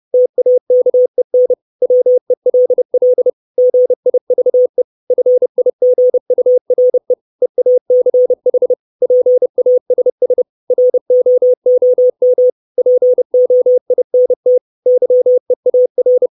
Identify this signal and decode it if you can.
Morse code, heard as a keyed tone, about 30 wpm, 500 Hz. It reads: TAKEN WELL GIVE FIGURE EACH PASS ROOM POINT YEAR
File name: 632 Words 30WPM